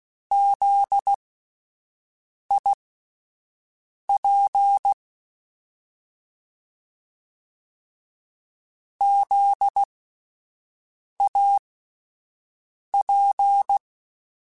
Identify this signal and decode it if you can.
Morse code, heard as a keyed tone, about 16 wpm, spaced as for 5 wpm, 800 Hz. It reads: ZIP ZAP
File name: Alphabetical / Alphabetical mp3s / Zs @5